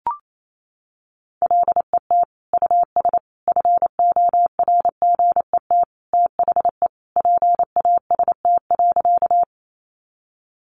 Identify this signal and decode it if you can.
Morse code, heard as a keyed tone, about 28 wpm, 700 Hz. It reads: LET US FORGET THE PAST.
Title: Sentences from Top 1200 Words 28wpm